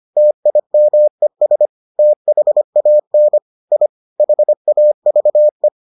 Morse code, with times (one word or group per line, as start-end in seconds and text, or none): 0.2-1.7 TIMES
2.0-3.4 THAN
3.7-3.9 I
4.2-5.7 HAVE